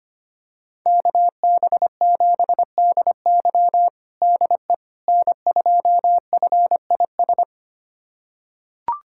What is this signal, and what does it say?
Morse code, heard as a keyed tone, about 25 wpm, 700 Hz. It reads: KB7DY DE N2FIS
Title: USA Callsign DE Callsign - 5 Characters - 25wpm